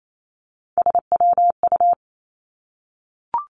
Morse code, heard as a keyed tone, about 28 wpm, 700 Hz. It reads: SWU
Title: Three Challenging Letter-Number Combinations 28wpm